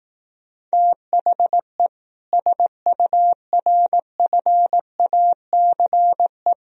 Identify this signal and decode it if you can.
Morse code, heard as a keyed tone, about 18 wpm, 700 Hz. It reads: THE SURFACE
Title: Speed-Racing - Sets of 4 Words 15wpm